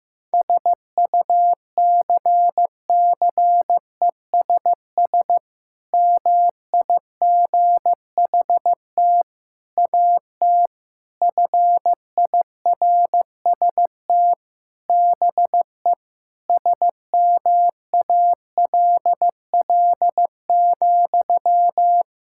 Morse code, heard as a keyed tone, about 15 wpm, 700 Hz. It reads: SUCCESS MIGHT AT FIRST BE SMALL,
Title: Sentences from Top 1000 Words - Review Part 6 - 15wpm